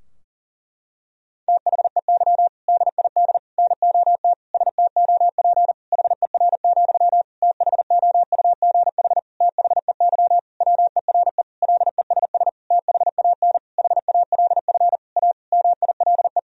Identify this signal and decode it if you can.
Morse code, heard as a keyed tone, about 37 wpm, 700 Hz. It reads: THEY DID NOT STOP HER, THOUGH THEY WERE LESS THAN HALF A MILE
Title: Sentences from Top 1000 Words - Review Part 2 - 40wpm